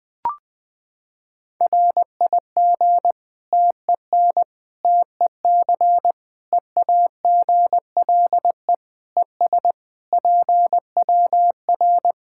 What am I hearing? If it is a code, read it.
Morse code, heard as a keyed tone, about 20 wpm, 700 Hz. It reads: RIG TEN TEC EAGLE ES PWR
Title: K1USN SST QSO Practice 20wpm